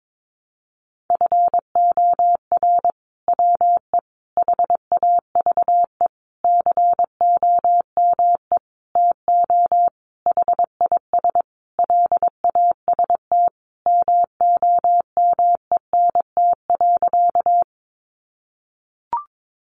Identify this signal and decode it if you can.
Morse code, heard as a keyed tone, about 22 wpm, 700 Hz. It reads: FOR WE HAVE COME TO HIS LAST MOMENT.
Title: Sentences from Top 1000 Words 22wpm